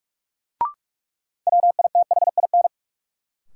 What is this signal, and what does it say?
Morse code, heard as a keyed tone, about 46 wpm, 700 Hz. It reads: WITHIN
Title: Sentences from Top 700 Words - New Word Drill - 45wpm